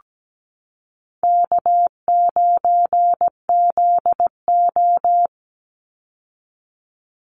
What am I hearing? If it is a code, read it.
Morse code, heard as a keyed tone, about 17 wpm, 700 Hz. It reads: K9ZO
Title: Easy Call Signs Repeated 2x - 17wpm